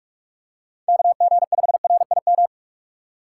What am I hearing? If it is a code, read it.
Morse code, heard as a keyed tone, about 47 wpm, 700 Hz. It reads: KG5RIM
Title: USA Call Signs - 3x Character Spacing - 45wpm